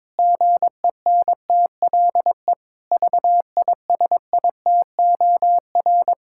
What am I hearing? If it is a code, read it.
Morse code, heard as a keyed tone, about 22 wpm, 700 Hz. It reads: GENTLE VISITOR
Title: Pangram Challenge v2 - 22wpm